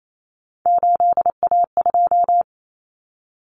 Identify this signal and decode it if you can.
Morse code, heard as a keyed tone, about 28 wpm, 700 Hz. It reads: 8A2
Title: Common Cut Numbers - 3-Digit 28wpm